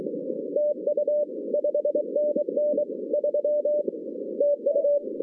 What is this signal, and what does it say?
Morse code, heard as a keyed tone, about 24 wpm, 580 Hz, with noise. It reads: TU 5NN 3 TU
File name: 2015 cq ww cw / 2015 cqwww k5tr - 0-18